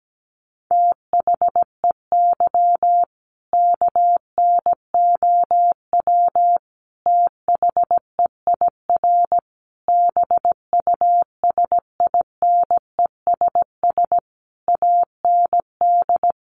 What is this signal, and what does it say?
Morse code, heard as a keyed tone, about 17 wpm, 700 Hz. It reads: THEY KNOW THEIR BUSINESS AND
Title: Sentences from Top 1200 Words 17wpm